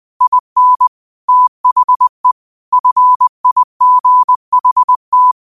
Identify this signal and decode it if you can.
Morse code, heard as a keyed tone, about 20 wpm, 1000 Hz. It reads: IN THE FIGHT
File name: qotdfiles / 20wpm